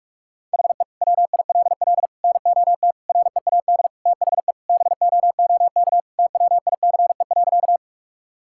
Morse code, heard as a keyed tone, about 46 wpm, 700 Hz. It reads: HE WILL NOT READ THE BOOK TWICE.
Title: Sentences from Top 1600 Words 45wpm